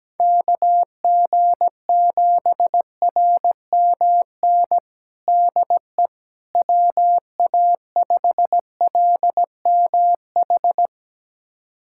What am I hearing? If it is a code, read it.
Morse code, heard as a keyed tone, about 17 wpm, 700 Hz. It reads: KG7RMN DE WA5LMH
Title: USA Callsign DE Callsign - 6 Characters - 17wpm